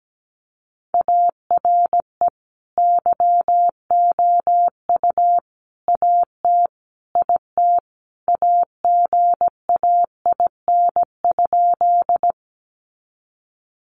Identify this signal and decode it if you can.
Morse code, heard as a keyed tone, about 17 wpm, 700 Hz. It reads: ARE YOU AT IT AGAIN?
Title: Sentences from Top 200 Words 17wpm